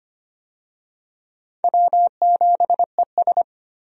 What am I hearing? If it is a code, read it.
Morse code, heard as a keyed tone, about 25 wpm, 700 Hz. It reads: W7ES
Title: USA Call Signs - 4 Characters - Repeated 2x - 25wpm